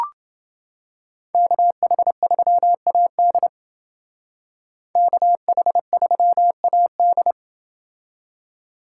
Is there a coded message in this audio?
Yes: Morse code from 1.3 to 7.3 s, heard as a keyed tone, about 28 wpm, 700 Hz. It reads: KH3AD KH3AD